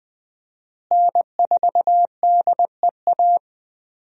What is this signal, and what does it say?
Morse code, heard as a keyed tone, about 20 wpm, 700 Hz. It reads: N4DEA
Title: USA Call Signs - 5 Characters - Repeated 3x - 20wpm